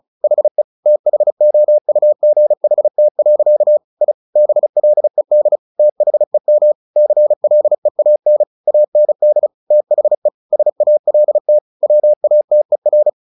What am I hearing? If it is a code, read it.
Morse code, heard as a keyed tone, about 34 wpm, 600 Hz. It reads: HE THOUGHT. I BLED THEM CLEAN AND THE SALT WATER
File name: the old man and the sea 2 35wpm